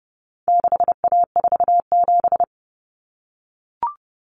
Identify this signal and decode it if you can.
Morse code, heard as a keyed tone, about 30 wpm, 700 Hz. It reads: 6A47
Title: Common Cut Numbers - 4-Digit 30wpm